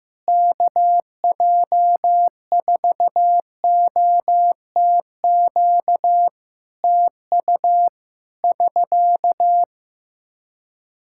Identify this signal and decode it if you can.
Morse code, heard as a keyed tone, about 15 wpm, 700 Hz. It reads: KJ4OTQ TU <SK>
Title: Practice QSO - Exchange 4 - Ending - 15wpm